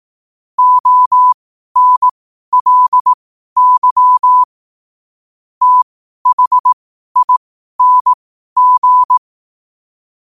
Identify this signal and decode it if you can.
Morse code, heard as a keyed tone, about 18 wpm, spaced as for 12 wpm, 1000 Hz. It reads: ONLY THING
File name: qotdfiles / 12wpm